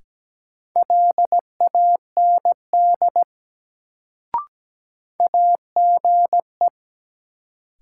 Morse code, heard as a keyed tone, about 17 wpm, 700 Hz. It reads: LAND AGE
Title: Sentences from Top 500 Words - New Word Drill - 17wpm